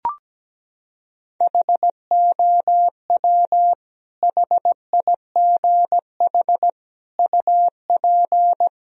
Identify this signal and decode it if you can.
Morse code, heard as a keyed tone, about 17 wpm, 700 Hz. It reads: HOW HIGH UP
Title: Sentences from Top 1000 Words - Review Part 1 - 17wpm